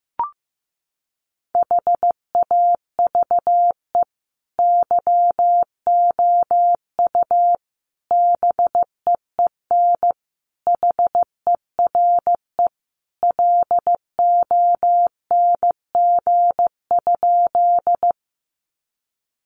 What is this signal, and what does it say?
Morse code, heard as a keyed tone, about 15 wpm, 700 Hz. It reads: HAVE YOU BEEN HERE LONG?
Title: Sentences from Top 200 Words 15wpm